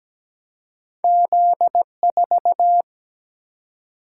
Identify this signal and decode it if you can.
Morse code, heard as a keyed tone, about 17 wpm, 700 Hz. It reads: Z4
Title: Single-Letter Single-Number 17wpm